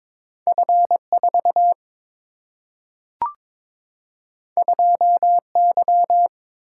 Morse code, heard as a keyed tone, about 22 wpm, 700 Hz. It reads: F 4 2 Y
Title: Single-Letter Single-Number 22wpm